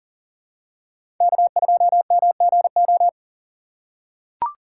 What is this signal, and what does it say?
Morse code, heard as a keyed tone, about 40 wpm, 700 Hz. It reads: K2MGO
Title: USA Call Signs - 5 Characters - 3x Character Spacing - 40wpm